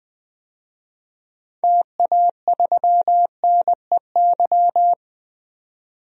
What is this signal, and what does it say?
Morse code, heard as a keyed tone, about 20 wpm, 700 Hz. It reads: TA3NEY
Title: DXCC-100 Call Signs - 6 Characters - 20wpm